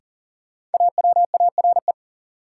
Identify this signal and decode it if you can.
Morse code, heard as a keyed tone, about 40 wpm, 700 Hz. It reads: AWARE
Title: Sentences from Top 1200 Words - New Word Drill Alt - 40wpm